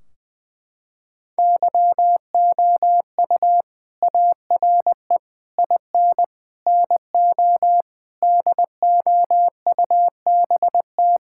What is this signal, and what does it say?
Morse code, heard as a keyed tone, about 20 wpm, 700 Hz. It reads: YOU ARE IN NO DOUBT
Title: Sentences from Top 1600 Words 20wpm